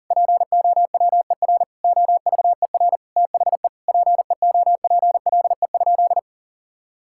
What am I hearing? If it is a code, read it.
Morse code, heard as a keyed tone, about 40 wpm, 700 Hz. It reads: POWER OVER THE PEOPLE?
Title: Sentences from Top 1000 Words - Review Part 5 - 40wpm